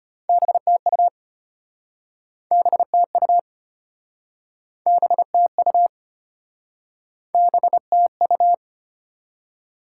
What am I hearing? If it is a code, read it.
Morse code, heard as a keyed tone, about 32 wpm, 690 Hz. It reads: BTU BTU BTU BTU